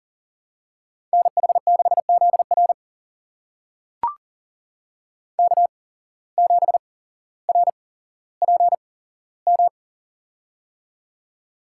Morse code, heard as a keyed tone, about 40 wpm, spaced as for 17 wpm, 700 Hz. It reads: NH6ZR K7RPM
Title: USA Call Signs - 8x Character Spacing - 40wpm